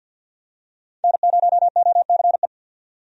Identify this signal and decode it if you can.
Morse code, heard as a keyed tone, about 50 wpm, 700 Hz. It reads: N0OCE
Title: USA Call Signs - 5 Characters - 50wpm